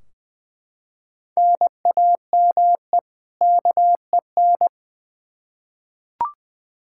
Morse code, heard as a keyed tone, about 20 wpm, 700 Hz. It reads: NAME KEN
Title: K1USN SST QSO Practice 20wpm